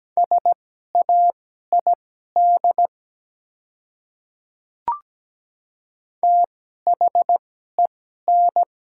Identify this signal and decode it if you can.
Morse code, heard as a keyed tone, about 17 wpm, spaced as for 12 wpm, 700 Hz. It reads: SAID THEN
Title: 2x Character Spacing - Top 100 Words - 17wpm